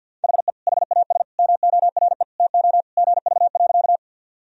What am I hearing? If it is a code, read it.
Morse code, heard as a keyed tone, about 51 wpm, 700 Hz. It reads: HE HAS MORE TO GV.
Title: Sentences from Top 200 Words and Abbreviations 50wpm